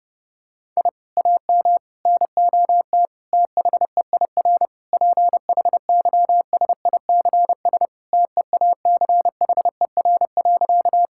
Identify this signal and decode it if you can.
Morse code, heard as a keyed tone, about 30 wpm, 700 Hz. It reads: I AM NOT THEIR PHYSICS TEACHER.
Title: Sentences from Top 1000 Words - Review Part 6 - 30wpm